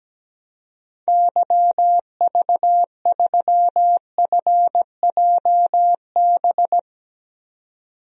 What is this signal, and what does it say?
Morse code, heard as a keyed tone, about 17 wpm, 700 Hz. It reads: YV3FJB
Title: DXCC-100 Call Signs - 6 Characters - 17wpm